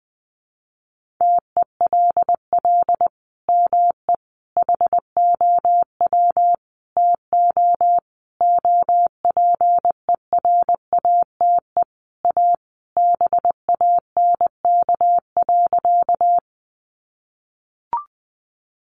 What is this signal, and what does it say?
Morse code, heard as a keyed tone, about 20 wpm, 700 Hz. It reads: TELL ME HOW TO OPERATE A BANK.
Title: Sentences from Top 1500 Words 20wpm